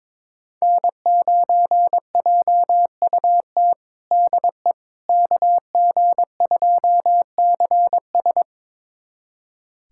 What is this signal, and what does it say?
Morse code, heard as a keyed tone, about 22 wpm, 700 Hz. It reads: N9JUT DE KG2CS
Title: USA Callsign DE Callsign - 5 Characters - 22wpm